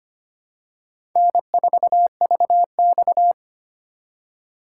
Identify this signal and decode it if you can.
Morse code, heard as a keyed tone, about 25 wpm, 700 Hz. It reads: N4VX
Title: USA Call Signs - 4 Characters - 4x Character Spacing - 25wpm